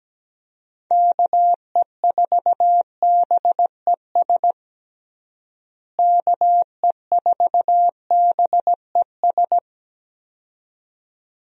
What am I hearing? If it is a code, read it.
Morse code, heard as a keyed tone, about 17 wpm, 700 Hz. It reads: KE4BES KE4BES
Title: USA Call Signs - 6 Characters - Repeated 2x - 17wpm